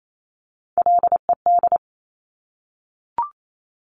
Morse code, heard as a keyed tone, about 28 wpm, 700 Hz. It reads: LED
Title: Top Three Letter Combinations in Words 28wpm